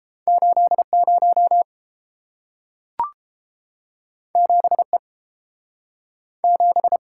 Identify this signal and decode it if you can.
Morse code, heard as a keyed tone, about 31 wpm, 700 Hz. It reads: 80 7E 7